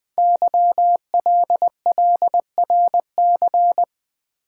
Morse code, heard as a keyed tone, about 20 wpm, 700 Hz. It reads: YLLRC
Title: Code Groups - 5 Letters - 20wpm